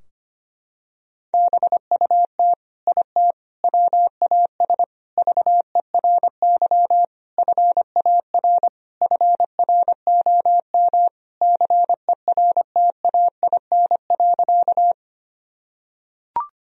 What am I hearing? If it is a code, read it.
Morse code, heard as a keyed tone, about 25 wpm, 700 Hz. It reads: BUT IT WAS VERY FAR FROM CERTAIN.
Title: Sentences from Top 1000 Words - Review Part 2 - 25wpm